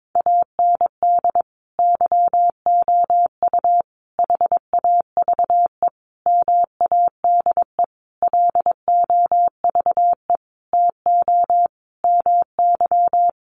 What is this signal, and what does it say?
Morse code, heard as a keyed tone, about 22 wpm, 700 Hz. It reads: AND YOU HAVE MADE LOVE TO MY
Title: Sentences from Top 1500 Words 22wpm